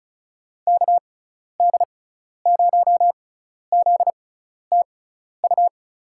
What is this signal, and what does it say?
Morse code, heard as a keyed tone, about 34 wpm, spaced as for 12 wpm, 700 Hz. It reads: KD0ZTU
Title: USA Call Signs - 6x Character Spacing - 35wpm